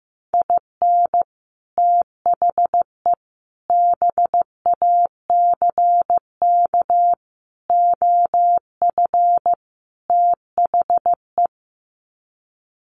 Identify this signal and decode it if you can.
Morse code, heard as a keyed tone, about 15 wpm, 700 Hz. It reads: IN THE BACK OF THE
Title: Sets of 5 Words 15wpm